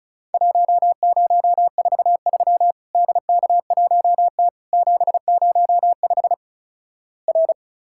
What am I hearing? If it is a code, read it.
Morse code, heard as a keyed tone, about 36 wpm, 700 Hz. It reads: 1043 DK1T 705 R
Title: Worked All Europe DX Contest - QTC Exchange - 35wpm